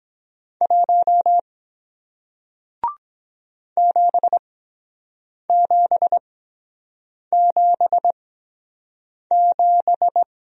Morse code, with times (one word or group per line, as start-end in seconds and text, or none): 0.6-1.4 1
3.8-10.2 7777